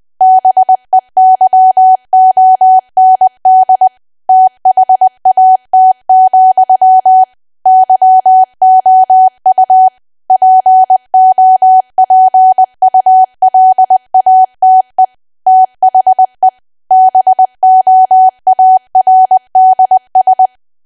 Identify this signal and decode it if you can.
Morse code, heard as a keyed tone, about 20 wpm, 750 Hz. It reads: BEYOND THAT, YOU POPULATE THE BOARDS
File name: cwa / W1AW-Level3 / 20WPM